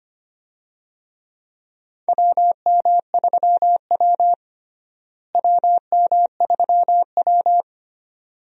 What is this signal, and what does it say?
Morse code, heard as a keyed tone, about 25 wpm, 700 Hz. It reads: WM3W WM3W
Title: USA Call Signs - 4 Characters - Repeated 2x - 25wpm